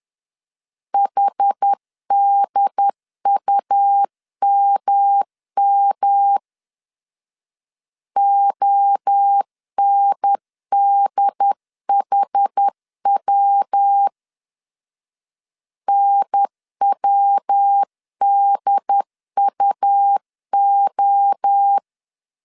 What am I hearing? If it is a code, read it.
Morse code, heard as a keyed tone, about 10 wpm, 800 Hz. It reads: HDUMM ONDHW NWDUO
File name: 10wpm